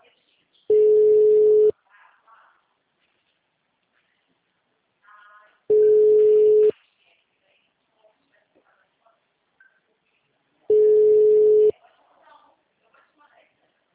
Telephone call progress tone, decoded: ringback tone

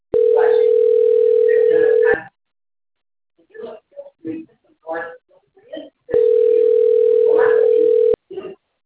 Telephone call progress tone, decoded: ringback tone

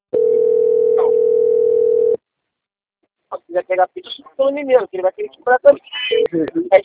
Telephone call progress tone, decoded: ringback tone

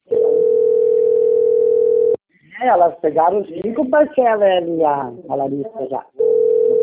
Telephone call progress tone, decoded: ringback tone

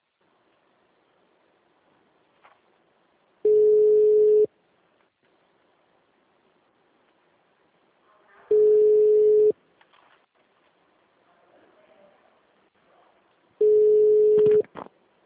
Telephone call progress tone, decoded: ringback tone

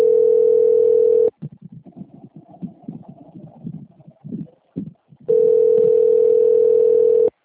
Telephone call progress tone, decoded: ringback tone

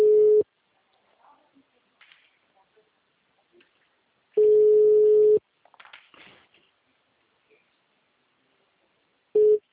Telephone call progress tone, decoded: ringback tone

